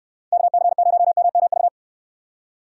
Morse code, heard as a keyed tone, about 69 wpm, 700 Hz. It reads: KC0MMV